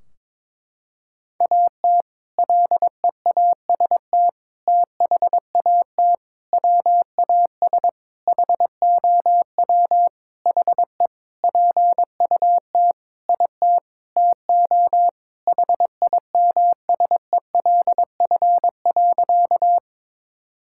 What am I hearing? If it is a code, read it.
Morse code, heard as a keyed tone, about 22 wpm, 700 Hz. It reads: AT LEAST THAT WAS HOW HE PUT IT TO HIMSELF.
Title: Sentences from Top 1600 Words 22wpm